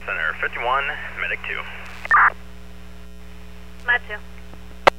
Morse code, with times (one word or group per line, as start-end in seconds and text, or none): none